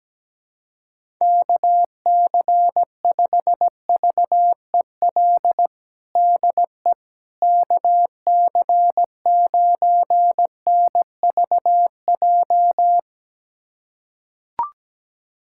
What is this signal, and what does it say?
Morse code, heard as a keyed tone, about 17 wpm, 700 Hz. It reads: KC5VEL DE KC9NVJ
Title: USA Callsign DE Callsign - 6 Characters - 17wpm